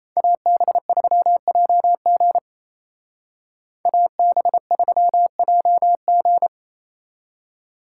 Morse code, heard as a keyed tone, about 34 wpm, 700 Hz. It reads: AB3JG AB3JG
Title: Speed Racing - USA 2x2 Call Signs 28wpm